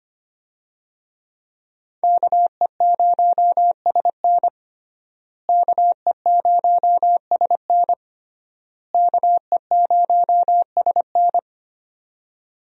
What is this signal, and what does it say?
Morse code, heard as a keyed tone, about 25 wpm, 700 Hz. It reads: KE0SN KE0SN KE0SN
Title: USA Call Signs - 5 Characters - Repeated 3x - 25wpm